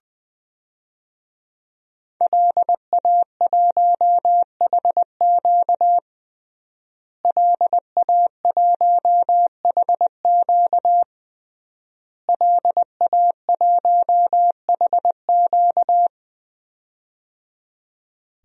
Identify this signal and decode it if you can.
Morse code, heard as a keyed tone, about 21 wpm, 700 Hz. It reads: LA1HQ LA1HQ LA1HQ
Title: DXCC-100 Call Signs - 5 Characters - Repeated 3x - 20wpm